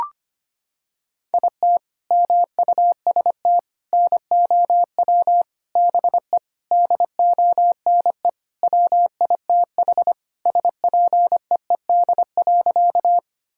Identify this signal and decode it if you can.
Morse code, heard as a keyed tone, about 25 wpm, 700 Hz. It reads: IT MUST NOW BE DONE WITH SPEED.
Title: Sentences from Top 1000 Words - Review Part 5 - 25wpm